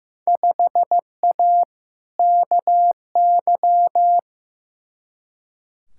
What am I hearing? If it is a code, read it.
Morse code, heard as a keyed tone, about 15 wpm, 700 Hz. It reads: 5A KY
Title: Field Day Exchanges - 15wpm